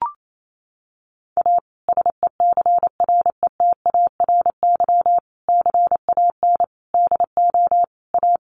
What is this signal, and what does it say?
Morse code, heard as a keyed tone, about 28 wpm, 700 Hz. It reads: A SECRETARY CAN DO A
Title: Sentences from Top 1300 Words 28wpm